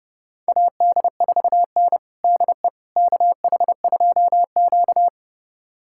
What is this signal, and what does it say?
Morse code, heard as a keyed tone, about 30 wpm, 700 Hz. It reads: AD4N DE KH2Q
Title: USA Callsign DE Callsign - 4 Characters - 30wpm